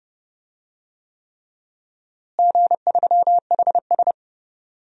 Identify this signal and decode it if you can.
Morse code, heard as a keyed tone, about 30 wpm, 700 Hz. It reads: G3HS
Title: DXCC-100 Call Signs - 4 Characters - Repeated 3x - 30wpm